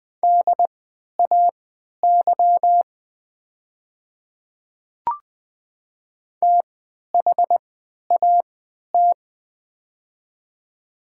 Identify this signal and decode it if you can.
Morse code, heard as a keyed tone, about 20 wpm, spaced as for 11 wpm, 700 Hz. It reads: DAY THAT
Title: 3x Character Spacing - Top 100 Words - 20wpm